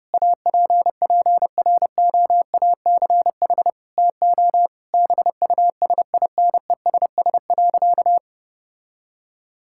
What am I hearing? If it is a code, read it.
Morse code, heard as a keyed tone, about 32 wpm, 700 Hz. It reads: APPROACH TO BUSINESS.
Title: Sentences from Top 1000 Words - Review Part 6 - 30wpm